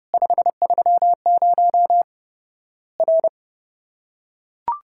Morse code, heard as a keyed tone, about 30 wpm, 700 Hz. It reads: 530 R
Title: Worked All Europe DX Contest - QTC Exchange - 30wpm